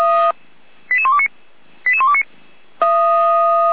Five-tone selective call (ZVEI), heard as quick runs of five tones, tones of about 70 ms. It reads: 88129 88129